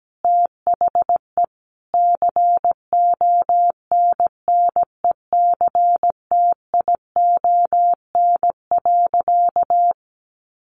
Morse code, heard as a keyed tone, about 17 wpm, 700 Hz. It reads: THE CONNECTION.
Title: Sentences from Top 1300 Words 17wpm